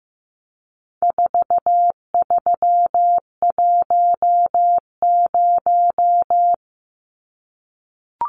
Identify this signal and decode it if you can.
Morse code, heard as a keyed tone, about 15 wpm, 700 Hz. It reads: 4310
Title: Numbers - 4-Digit 15wpm